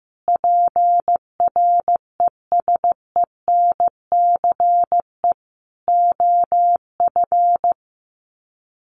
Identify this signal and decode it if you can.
Morse code, heard as a keyed tone, about 15 wpm, 700 Hz. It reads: PRESENCE OF
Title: Sets of 3 Words 15wpm